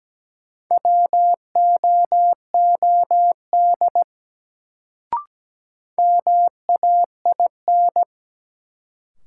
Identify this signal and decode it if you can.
Morse code, heard as a keyed tone, about 17 wpm, 700 Hz. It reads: WOOD MAIN
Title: Sentences from Top 1000 Words - Review Part 2 - New Word Drill - 17wpm